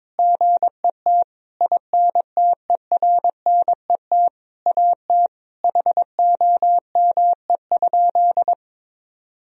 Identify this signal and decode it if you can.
Morse code, heard as a keyed tone, about 21 wpm, 700 Hz. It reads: GET INTERNET AT HOME?
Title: Sentences from Top 1000 Words - Review Part 4 - 22wpm